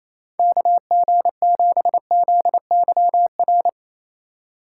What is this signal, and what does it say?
Morse code, heard as a keyed tone, about 36 wpm, 700 Hz. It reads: KG7ZYR